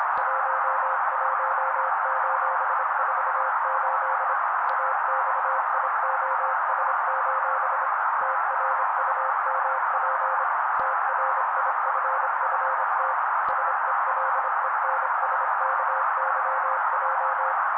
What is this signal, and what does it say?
Morse code, heard as a keyed tone, about 26 wpm, 530 Hz. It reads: 11:48 AXIOS: TRUMP TRIFFT SELENSKYJ
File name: daily / 25wpm noise